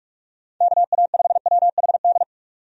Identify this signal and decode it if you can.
Morse code, heard as a keyed tone, about 42 wpm, 700 Hz. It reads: KA5WHD
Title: USA Call Signs - 6 Characters - 45wpm